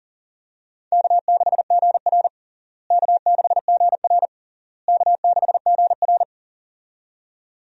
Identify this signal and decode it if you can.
Morse code, heard as a keyed tone, about 37 wpm, 700 Hz. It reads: K6GR K6GR K6GR
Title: USA Call Signs - 4 Characters - Repeated 3x - 40wpm